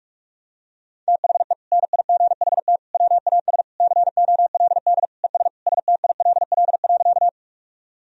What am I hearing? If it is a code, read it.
Morse code, heard as a keyed tone, about 46 wpm, 700 Hz. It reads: THE NIGHT WAS COLD ES STILL.
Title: Sentences from Top 400 Words and Abbreviations 45wpm